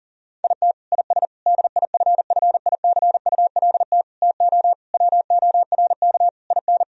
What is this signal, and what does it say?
Morse code, heard as a keyed tone, about 37 wpm, 700 Hz. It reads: IT IS DIFFICULT TO WORK IN
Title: Sentences from Top 1000 Words - Review Part 3 - 40wpm